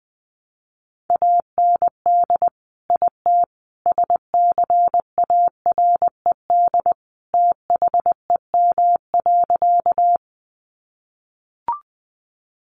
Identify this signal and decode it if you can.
Morse code, heard as a keyed tone, about 20 wpm, 700 Hz. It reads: AND IT SCARED THEM.
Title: Sentences from Top 700 Words 20wpm